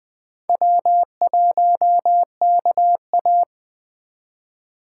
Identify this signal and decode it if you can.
Morse code, heard as a keyed tone, about 20 wpm, 700 Hz. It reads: W1KA
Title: Speed Racing - USA Call Signs 17wpm